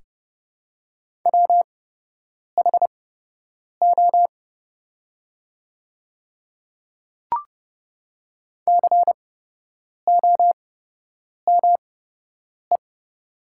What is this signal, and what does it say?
Morse code, heard as a keyed tone, about 30 wpm, spaced as for 8 wpm, 700 Hz. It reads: WHO COME
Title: 8x Character Spacing - Top 100 Words - 30wpm